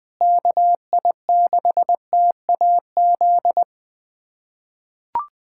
Morse code, heard as a keyed tone, about 20 wpm, 700 Hz. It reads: KI6TAZ
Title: USA Callsign DE Callsign 20wpm